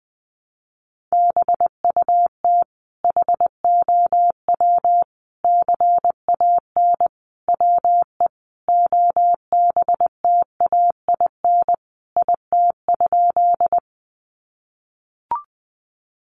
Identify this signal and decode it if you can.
Morse code, heard as a keyed tone, about 19 wpm, 700 Hz. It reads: BUT HOW CAN WE OBTAIN IT?
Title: Sentences from Top 1300 Words 20wpm